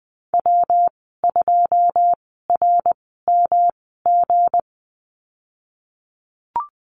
Morse code, heard as a keyed tone, about 20 wpm, spaced as for 15 wpm, 700 Hz. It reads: W2RMG
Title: 173 Lesson 28g - Number 9 - Call Signs 20wpm